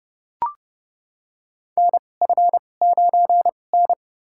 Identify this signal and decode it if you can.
Morse code, heard as a keyed tone, about 32 wpm, spaced as for 22 wpm, 700 Hz. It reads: NF9N
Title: USA Call Signs - 2x Character Spacing - 30wpm